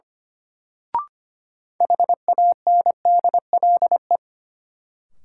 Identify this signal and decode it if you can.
Morse code, heard as a keyed tone, about 25 wpm, 700 Hz. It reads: HANDLE
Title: Sentences from Top 1100 Words - New Word Drill Alt - 25wpm